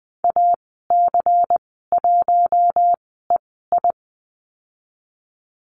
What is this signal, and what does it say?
Morse code, heard as a keyed tone, about 20 wpm, spaced as for 15 wpm, 700 Hz. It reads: AC1EI
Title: 140 Lesson 23g - Letter f - Call Signs 20wpm